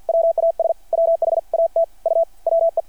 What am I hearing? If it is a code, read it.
Morse code, heard as a keyed tone, about 50 wpm, 660 Hz. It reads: WAS WHAT U WE
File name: archived / CRSnet